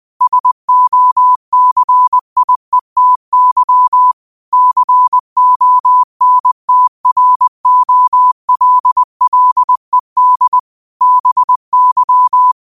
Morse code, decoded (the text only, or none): SOCIETY CONTROLLED BY